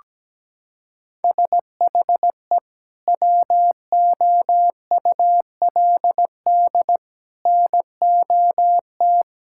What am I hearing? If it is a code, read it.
Morse code, heard as a keyed tone, about 17 wpm, 700 Hz. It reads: SHE WOULD NOT
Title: Sentences from Top 1000 Words - Review Part 6 - 17wpm